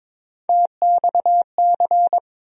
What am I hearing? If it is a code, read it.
Morse code, heard as a keyed tone, about 23 wpm, 700 Hz. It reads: TXC
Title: USA Call Sign Postfixes - 3 Letters 22wpm